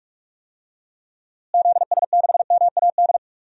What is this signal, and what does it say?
Morse code, heard as a keyed tone, about 46 wpm, 700 Hz. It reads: ZS6MAD
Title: DXCC-100 Call Signs - 6 Characters - Repeated 2x - 45wpm